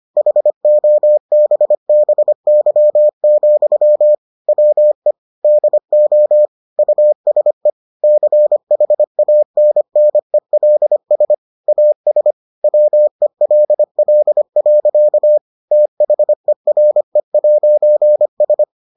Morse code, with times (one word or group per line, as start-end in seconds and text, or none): 0.2-4.1 HOBBY,
4.5-5.1 WE
5.4-6.5 DO
6.8-7.7 USE
8.0-11.3 CHANNELS
11.7-12.3 AS
12.6-15.4 WELL.
15.7-18.6 THERE'S